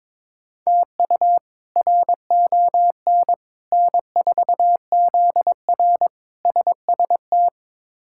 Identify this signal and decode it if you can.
Morse code, heard as a keyed tone, about 22 wpm, 700 Hz. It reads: TU RON N4ZR SST
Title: K1USN SST QSO Practice - 22wpm